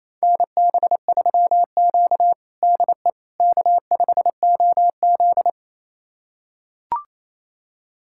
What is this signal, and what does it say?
Morse code, heard as a keyed tone, about 28 wpm, 700 Hz. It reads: NB3Q DE K5OZ
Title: USA Callsign DE Callsign - 4 Characters - 28wpm